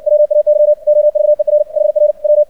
Morse code, heard as a keyed tone, about 56 wpm, 600 Hz. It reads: WAY OVER HR W
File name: archived / CRSnet